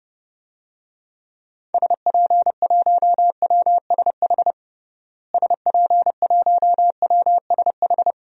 Morse code, heard as a keyed tone, about 32 wpm, 700 Hz. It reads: SP1WSH SP1WSH
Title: DXCC-100 Call Signs - 6 Characters - Repeated 3x - 30wpm